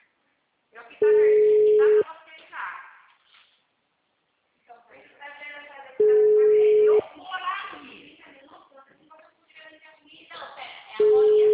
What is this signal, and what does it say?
Ringback tone (Europe, 425 Hz), with speech.